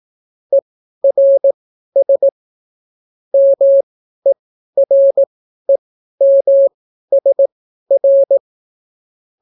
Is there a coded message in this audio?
Yes: Morse code from 0.5 to 8.4 s, heard as a keyed tone, about 18 wpm, spaced as for 12 wpm, 550 Hz. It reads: ERS MEREMSR